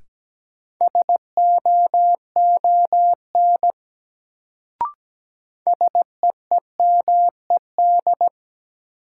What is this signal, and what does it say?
Morse code, heard as a keyed tone, about 17 wpm, 700 Hz. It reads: SOON SEEMED
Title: Sets of 3 Words - New Word Drill - 17wpm